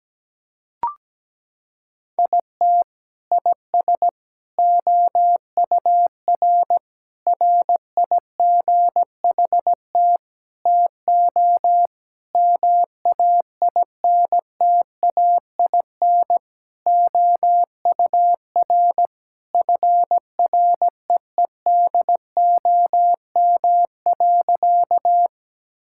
Morse code, heard as a keyed tone, about 17 wpm, 700 Hz. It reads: IT IS OUR RIGHT TO MAINTAIN OUR FREEDOM.
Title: Sentences from Top 1000 Words - Review Part 5 - 17wpm